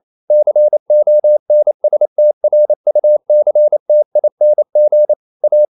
Morse code, heard as a keyed tone, about 28 wpm, 600 Hz. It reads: CONSTRUCTING A
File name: CW / Practice / EnglishQuotations0228